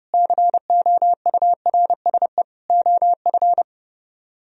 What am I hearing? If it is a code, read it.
Morse code, heard as a keyed tone, about 38 wpm, 700 Hz. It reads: COURSE OF